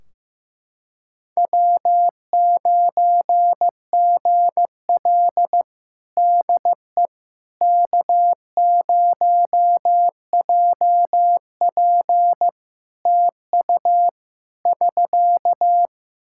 Morse code, heard as a keyed tone, about 15 wpm, 700 Hz. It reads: W9GL DE K0JP TU <SK>
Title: Practice QSO - Exchange 4 - Ending - 15wpm